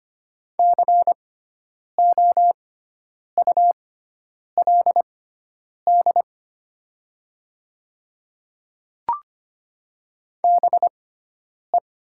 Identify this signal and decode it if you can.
Morse code, heard as a keyed tone, about 25 wpm, spaced as for 9 wpm, 700 Hz. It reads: COULD BE
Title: 6x Character Spacing - Top 100 Words - 25wpm